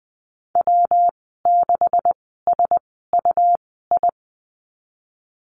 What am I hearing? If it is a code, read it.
Morse code, heard as a keyed tone, about 20 wpm, spaced as for 14 wpm, 700 Hz. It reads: W6SUI
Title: 128 Lesson 21g - Number 6 - Call Signs 20wpm